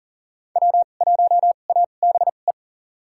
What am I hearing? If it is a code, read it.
Morse code, heard as a keyed tone, about 40 wpm, spaced as for 29 wpm, 700 Hz. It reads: W1ABE
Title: USA Call Signs - 5 Characters - 2x Character Spacing - 40wpm